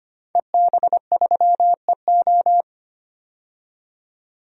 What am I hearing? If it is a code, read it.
Morse code, heard as a keyed tone, about 25 wpm, 700 Hz. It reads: EB3EO
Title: DXCC-100 Call Signs - 5 Characters - 25wpm